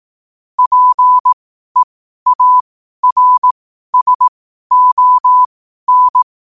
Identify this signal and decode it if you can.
Morse code, heard as a keyed tone, about 18 wpm, spaced as for 12 wpm, 1000 Hz. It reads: PEARSON
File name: qotdfiles / 12wpm